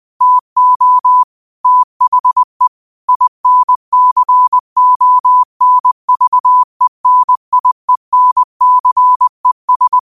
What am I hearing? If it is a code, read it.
Morse code, heard as a keyed tone, about 20 wpm, 1000 Hz. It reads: TO THE INCONVENIENCES